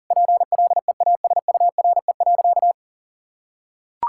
Morse code, heard as a keyed tone, about 40 wpm, 700 Hz. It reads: PLEASURE.
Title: Sentences from Top 1600 Words 40wpm